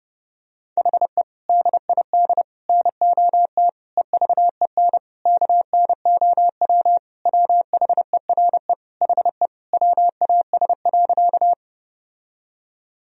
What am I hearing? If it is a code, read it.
Morse code, heard as a keyed tone, about 32 wpm, 700 Hz. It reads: HE DID NOT EVEN KNOW WHERE HE WAS.
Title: Sentences from Top 1000 Words - Review Part 1 - 30wpm